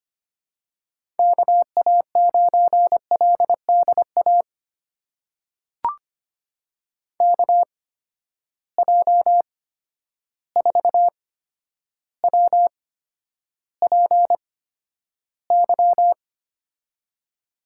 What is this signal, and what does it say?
Morse code, heard as a keyed tone, about 25 wpm, spaced as for 11 wpm, 700 Hz. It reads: KA9LDA KJ4WPY
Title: USA Call Signs - 6 Characters - 8x Character Spacing - 25wpm